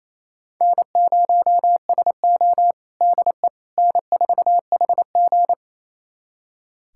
Morse code, heard as a keyed tone, about 28 wpm, 700 Hz. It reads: N0SO DE N4HG
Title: USA Callsign DE Callsign - 4 Characters - 28wpm